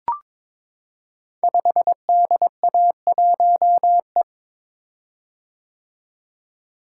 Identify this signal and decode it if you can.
Morse code, heard as a keyed tone, about 22 wpm, 700 Hz. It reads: 5DA1E
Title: Code Groups - 5 Letter:Numbers - 22wpm